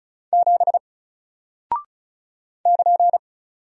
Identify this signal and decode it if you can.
Morse code, heard as a keyed tone, about 34 wpm, 700 Hz. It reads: 7 <KN>
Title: Single Character (Letter:Number:Punctuation:Prosign) Alt 35wpm